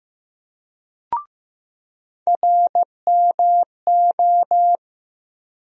Morse code, heard as a keyed tone, about 15 wpm, 700 Hz. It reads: RMO
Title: USA Call Sign Postfixes - 3 Letters 15wpm